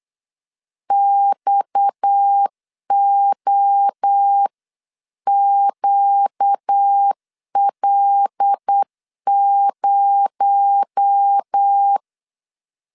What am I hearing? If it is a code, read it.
Morse code, heard as a keyed tone, about 8 wpm, 800 Hz. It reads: XOQL0
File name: 8wpm